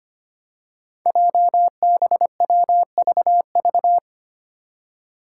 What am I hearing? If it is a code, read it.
Morse code, heard as a keyed tone, about 25 wpm, 700 Hz. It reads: JBWVV
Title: Five Challenging Letter-Number Combinations 25wpm